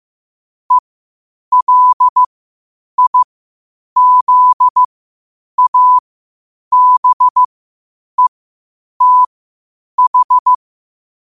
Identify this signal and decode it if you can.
Morse code, heard as a keyed tone, about 15 wpm, spaced as for 8 wpm, 1000 Hz. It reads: ELIZABETH
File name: qotdfiles / 8wpm